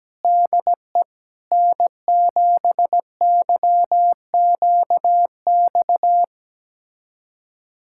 Morse code, heard as a keyed tone, about 17 wpm, 700 Hz. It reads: DE N7YQX
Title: USA Callsign DE Callsign - 5 Characters - 17wpm